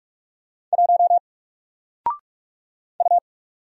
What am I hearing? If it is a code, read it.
Morse code, heard as a keyed tone, about 44 wpm, 700 Hz. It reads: 1 U
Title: Single Letter-Number Alt 45wpm